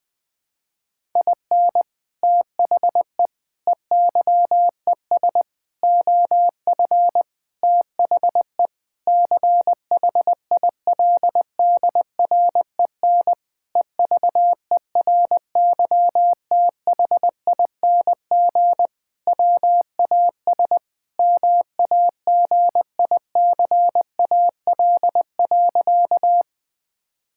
Morse code, heard as a keyed tone, about 20 wpm, 700 Hz. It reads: IN THE EYES OF THE CHILDREN EVERYTHING WAS MAGICAL.
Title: Sentences with 5-Word Phrases - 20wpm